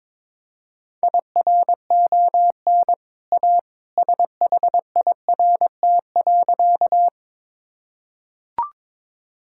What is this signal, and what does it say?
Morse code, heard as a keyed tone, about 22 wpm, 700 Hz. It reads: IRON A SHIRT.
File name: practice / 22wpm / Easy Sentences with Hints - 22wpm